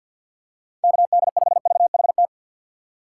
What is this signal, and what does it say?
Morse code, heard as a keyed tone, about 51 wpm, 700 Hz. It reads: KD5VHT
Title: USA Call Signs - 6 Characters - 4x Character Spacing - 50wpm